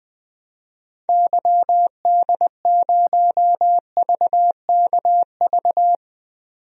Morse code, heard as a keyed tone, about 20 wpm, 700 Hz. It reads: YD0VKV
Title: DXCC-100 Call Signs - 6 Characters - 20wpm